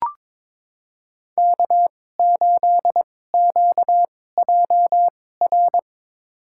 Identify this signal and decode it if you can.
Morse code, heard as a keyed tone, about 22 wpm, spaced as for 16 wpm, 700 Hz. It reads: K8QJR
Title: USA Call Signs - 5 Characters - 2x Character Spacing - 22wpm